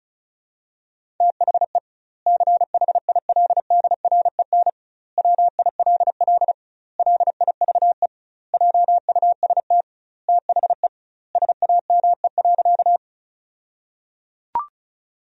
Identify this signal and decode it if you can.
Morse code, heard as a keyed tone, about 34 wpm, 700 Hz. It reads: THE CHILDREN WILL LIVE JUST THE SAME.
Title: 2x Word-Spacing - Sentences from Top 300 Words - 35wpm